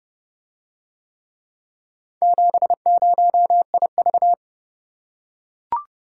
Morse code, heard as a keyed tone, about 29 wpm, 700 Hz. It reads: 70IV